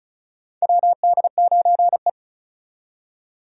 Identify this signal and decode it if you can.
Morse code, heard as a keyed tone, about 34 wpm, 700 Hz. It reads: WD9E
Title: USA Call Signs - 4 Characters - 35wpm